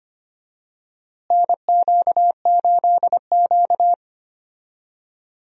Morse code, heard as a keyed tone, about 25 wpm, 700 Hz. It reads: NQ8Q
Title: USA Call Signs - 4 Characters - Repeated 3x - 25wpm